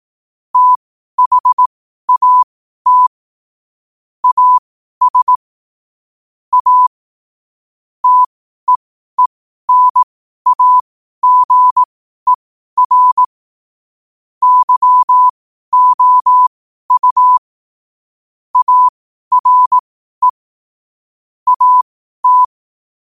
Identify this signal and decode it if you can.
Morse code, heard as a keyed tone, about 18 wpm, spaced as for 12 wpm, 1000 Hz. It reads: THAT AS A TEENAGER YOU ARE AT
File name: qotdfiles / 12wpm